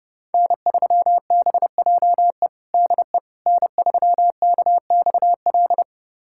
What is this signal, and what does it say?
Morse code, heard as a keyed tone, about 30 wpm, 700 Hz. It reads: N3BJE DE N3KXL
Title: USA Callsign DE Callsign - 5 Characters - 30wpm